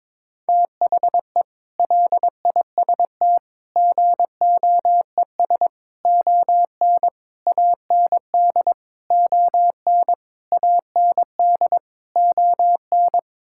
Morse code, heard as a keyed tone, about 22 wpm, 700 Hz. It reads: THE LIST GOES ON AND ON AND ON
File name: practice / 22wpm / Sets of 8 Words - 22wpm